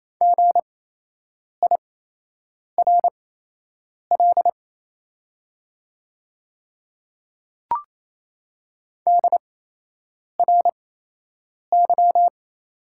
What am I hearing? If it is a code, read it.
Morse code, heard as a keyed tone, about 28 wpm, spaced as for 8 wpm, 690 Hz. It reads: GIRL DRY
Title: 8x Character Spacing - Top 500 Words - 28wpm